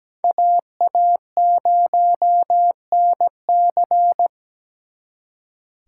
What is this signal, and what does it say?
Morse code, heard as a keyed tone, about 20 wpm, 700 Hz. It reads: AA0NC
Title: Speed Racing - USA Call Signs 17wpm